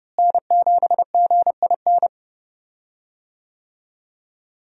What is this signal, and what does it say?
Morse code, heard as a keyed tone, about 31 wpm, 700 Hz. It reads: N7GIN
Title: USA Call Signs - 5 Characters - Repeated 3x - 30wpm